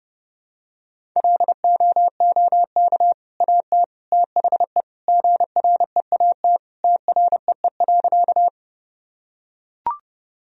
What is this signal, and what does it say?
Morse code, heard as a keyed tone, about 32 wpm, 700 Hz. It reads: LOOK AT THE GREAT TREE.
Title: Sentences from Top 1000 Words - Review Part 1 - 30wpm